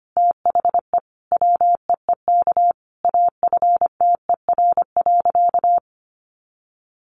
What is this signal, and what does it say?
Morse code, heard as a keyed tone, about 25 wpm, 700 Hz. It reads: THE WEEK AFTER.
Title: Sentences from Top 500 Words 25wpm